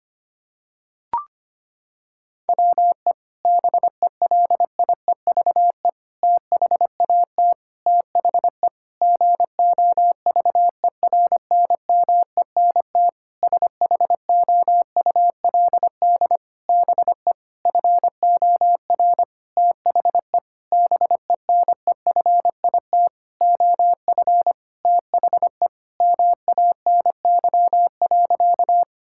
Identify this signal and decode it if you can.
Morse code, heard as a keyed tone, about 25 wpm, 700 Hz. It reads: WE BELIEVE THAT THE GOVERNMENT SHOULD BE FOR THE BENEFIT OF THE MANY.
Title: Sentences from Top 1000 Words - Review Part 6 - 25wpm